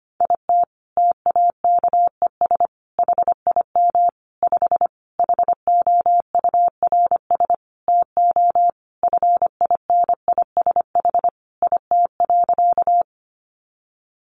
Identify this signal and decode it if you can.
Morse code, heard as a keyed tone, about 25 wpm, 700 Hz. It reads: IT TAKES HIM 5 HOURS TO FINISH IT.
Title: Additional Easy Phrases 25wpm